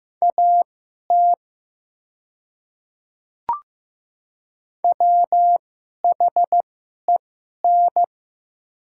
Morse code, heard as a keyed tone, about 15 wpm, spaced as for 11 wpm, 700 Hz. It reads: AT WHEN
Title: 2x Character Spacing - Top 100 Words - 15wpm